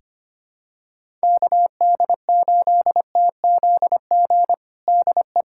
Morse code, heard as a keyed tone, about 25 wpm, 700 Hz. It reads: KD8TZG DE
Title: USA Callsign DE Callsign - 6 Characters - 25wpm